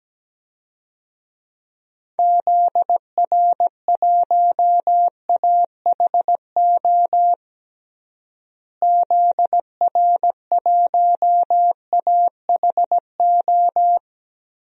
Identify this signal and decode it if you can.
Morse code, heard as a keyed tone, about 17 wpm, 700 Hz. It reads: ZR1AHO ZR1AHO
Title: DXCC-100 Call Signs - 6 Characters - Repeated 2x - 17wpm